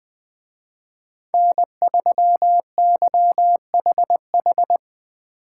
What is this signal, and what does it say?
Morse code, heard as a keyed tone, about 20 wpm, 700 Hz. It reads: N3YHH